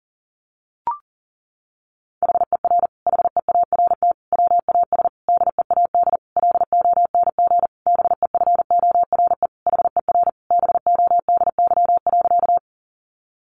Morse code, heard as a keyed tone, about 40 wpm, 700 Hz. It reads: HER HEART WAS DEAD LONG BEFORE HER BODY.
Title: Sentences from Top 1500 Words 40wpm